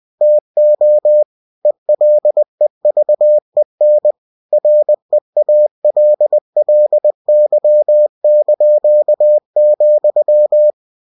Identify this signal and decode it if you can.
Morse code, heard as a keyed tone, about 20 wpm, 600 Hz. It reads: TO ELEVEN REALLY),
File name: cw-books / winnie-the-pooh / winnie-20wpm-page-0032